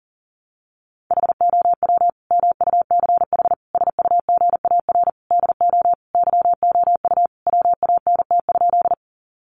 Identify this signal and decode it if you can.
Morse code, heard as a keyed tone, about 40 wpm, 700 Hz. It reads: HOW MUCH SUGAR DO YOU WANT?
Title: Sentences from Top 1200 Words 40wpm